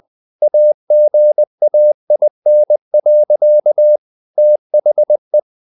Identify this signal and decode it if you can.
Morse code, heard as a keyed tone, about 20 wpm, 600 Hz. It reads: AGAIN. THE
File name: the old man and the sea part 2 20wpm